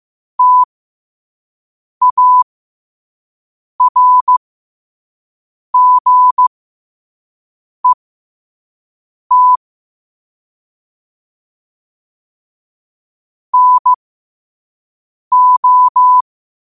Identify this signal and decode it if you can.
Morse code, heard as a keyed tone, about 15 wpm, spaced as for 5 wpm, 1000 Hz. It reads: TARGET NO